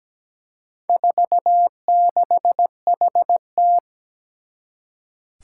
Morse code, heard as a keyed tone, about 17 wpm, 700 Hz. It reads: 46HT
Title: Four Challenging Letter-Number Combinations 17wpm